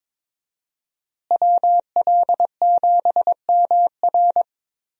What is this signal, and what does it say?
Morse code, heard as a keyed tone, about 22 wpm, 700 Hz. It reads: WL7MR
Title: USA Call Signs - 5 Characters - Repeated 3x - 22wpm